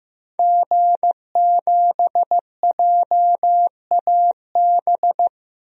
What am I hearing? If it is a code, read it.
Morse code, heard as a keyed tone, about 15 wpm, 700 Hz. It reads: G7JAB
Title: DXCC-100 Call Signs - 5 Characters - Repeated 2x - 15wpm